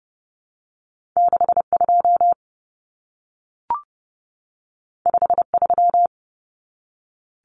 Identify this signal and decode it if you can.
Morse code, heard as a keyed tone, about 30 wpm, 700 Hz. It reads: 62 53
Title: Common Cut Numbers - 2-Digit 30wpm